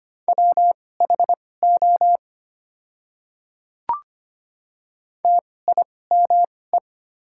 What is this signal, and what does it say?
Morse code, heard as a keyed tone, about 25 wpm, spaced as for 18 wpm, 690 Hz. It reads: WHO TIME
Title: 2x Character Spacing - Top 200 Words - 25wpm